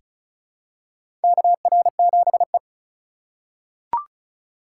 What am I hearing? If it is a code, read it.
Morse code, heard as a keyed tone, about 36 wpm, 700 Hz. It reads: KR7E
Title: USA Call Signs - 4 Characters - 3x Character Spacing - 35wpm